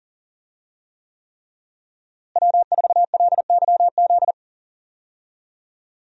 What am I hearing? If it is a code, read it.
Morse code, heard as a keyed tone, about 40 wpm, 700 Hz. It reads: W4LYZ
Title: USA Call Signs - 5 Characters - Repeated 3x - 40wpm